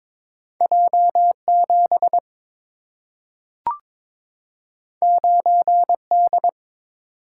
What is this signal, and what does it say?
Morse code, heard as a keyed tone, about 23 wpm, 690 Hz. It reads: J7 9D
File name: practice / 22wpm / Two Letter-Number Combinations - 22wpm